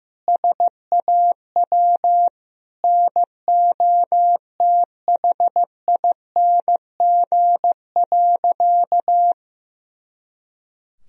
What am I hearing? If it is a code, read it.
Morse code, heard as a keyed tone, about 15 wpm, 700 Hz. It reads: SAW NOTHING.
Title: Sentences from Top 500 Words and Abbreviations 15wpm